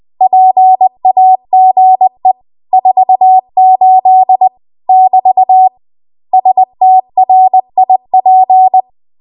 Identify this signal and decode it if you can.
Morse code, heard as a keyed tone, about 20 wpm, 750 Hz. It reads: PAGE 48 <BT> STRIP
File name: Morse / 20WPM